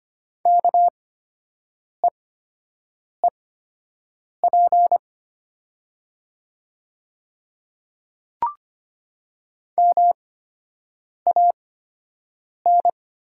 Morse code, heard as a keyed tone, about 25 wpm, spaced as for 7 wpm, 700 Hz. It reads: KEEP MAN
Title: 8x Character Spacing - Top 500 Words - 25wpm